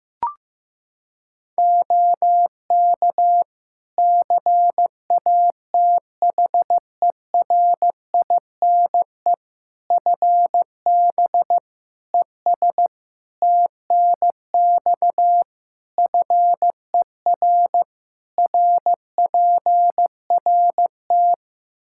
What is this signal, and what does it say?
Morse code, heard as a keyed tone, about 15 wpm, 700 Hz. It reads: OK CATHERINE FB ES TNX FER RPRT
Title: Practice QSO - Exchange 2 - Info - 15wpm